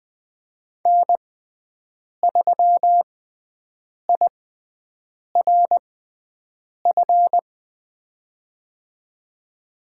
Morse code, heard as a keyed tone, about 21 wpm, spaced as for 7 wpm, 700 Hz. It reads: N3IRF
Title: USA Call Signs - 6x Character Spacing - 20wpm